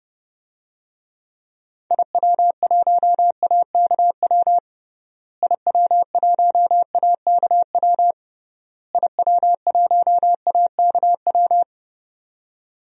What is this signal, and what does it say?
Morse code, heard as a keyed tone, about 32 wpm, 700 Hz. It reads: IW1AKW IW1AKW IW1AKW
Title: DXCC-100 Call Signs - 6 Characters - Repeated 3x - 30wpm